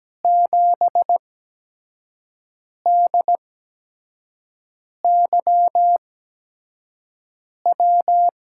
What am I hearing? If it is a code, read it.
Morse code, heard as a keyed tone, about 17 wpm, 700 Hz. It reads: 7 D Y W
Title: USA Call Signs - 6 Characters - 8x Character Spacing - 17wpm